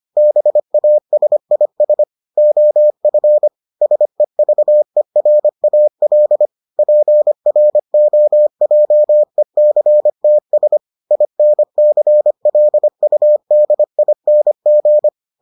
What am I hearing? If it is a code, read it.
Morse code, heard as a keyed tone, about 25 wpm, 600 Hz. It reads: BASIS OF SEVERAL PROJECTS INCLUDING